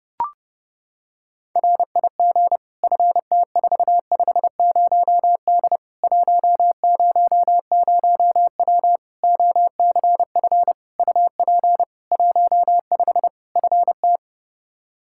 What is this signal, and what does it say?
Morse code, heard as a keyed tone, about 30 wpm, 700 Hz. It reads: RIG FT450D 100W OCF UP 15 FT
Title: Practice QSO - Exchange 2 - Info - 30wpm